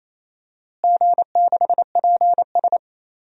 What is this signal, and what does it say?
Morse code, heard as a keyed tone, about 28 wpm, 700 Hz. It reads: G6PS
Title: DXCC-100 Call Signs - 4 Characters - Repeated 2x - 28wpm